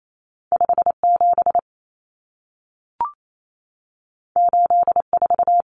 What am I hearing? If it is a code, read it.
Morse code, heard as a keyed tone, about 28 wpm, 700 Hz. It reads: 57 84
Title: Common Cut Numbers - 2-Digit 28wpm